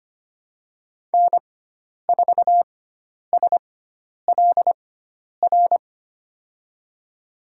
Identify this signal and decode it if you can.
Morse code, heard as a keyed tone, about 25 wpm, spaced as for 10 wpm, 700 Hz. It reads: N4SLR
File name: Lesson 16g - Letter c - Call Signs 25wpm